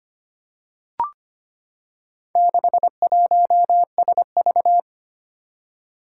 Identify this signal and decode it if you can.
Morse code, heard as a keyed tone, about 25 wpm, 700 Hz. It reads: 61SV
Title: Four Challenging Letter-Number Combinations 25wpm